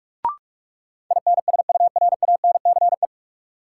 Morse code, heard as a keyed tone, about 46 wpm, 700 Hz. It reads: INSURANCE
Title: Sentences from Top 1100 Words - New Word Drill Alt - 45wpm